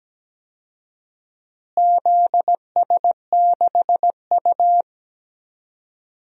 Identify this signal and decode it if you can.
Morse code, heard as a keyed tone, about 17 wpm, 700 Hz. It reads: ZS6U
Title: DXCC-100 Call Signs - 4 Characters - Repeated 2x - 17wpm